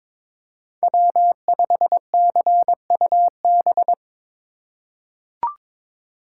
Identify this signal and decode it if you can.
Morse code, heard as a keyed tone, about 22 wpm, 700 Hz. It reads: W5CUB
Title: USA Call Signs - 4x Character Spacing - 22wpm